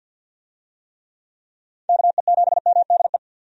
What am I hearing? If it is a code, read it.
Morse code, heard as a keyed tone, about 50 wpm, 700 Hz. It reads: KE7MDE
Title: USA Call Signs - 6 Characters - 50wpm